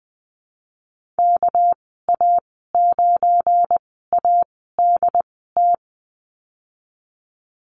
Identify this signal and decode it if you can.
Morse code, heard as a keyed tone, about 20 wpm, spaced as for 15 wpm, 700 Hz. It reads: KA9ADT
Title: 245 Lesson 39g - Letter z - Call Signs 20wpm